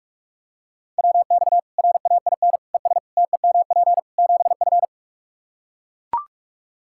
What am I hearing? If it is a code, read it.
Morse code, heard as a keyed tone, about 42 wpm, 700 Hz. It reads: WX RAIN ES TEMP 7F
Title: Practice QSO - Exchange 2 - Info - 45wpm